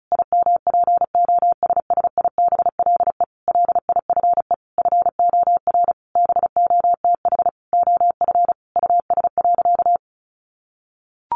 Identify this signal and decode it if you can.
Morse code, heard as a keyed tone, about 34 wpm, 700 Hz. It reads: IMPOSSIBLE LIFE FOR BOTH OF US.
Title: Sentences from Top 1100 Words 35wpm